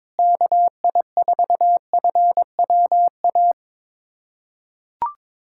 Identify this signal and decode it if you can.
Morse code, heard as a keyed tone, about 22 wpm, 700 Hz. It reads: KI4FWA
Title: USA Callsign DE Callsign - 6 Characters - 22wpm